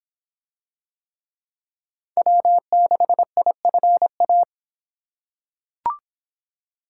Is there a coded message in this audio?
Yes: Morse code from 2.2 to 4.4 s, heard as a keyed tone, about 24 wpm, 700 Hz. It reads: W6IFA